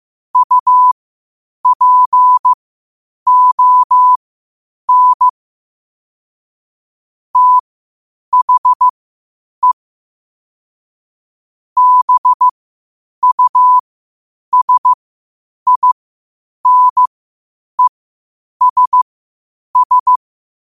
Morse code, heard as a keyed tone, about 15 wpm, spaced as for 8 wpm, 1000 Hz. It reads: UPON THE BUSINESS
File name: qotdfiles / 8wpm